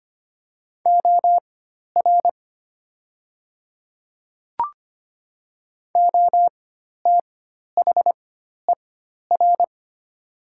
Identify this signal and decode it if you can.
Morse code, heard as a keyed tone, about 25 wpm, spaced as for 12 wpm, 700 Hz. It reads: OR OTHER
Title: 4x Character Spacing - Top 100 Words - 25wpm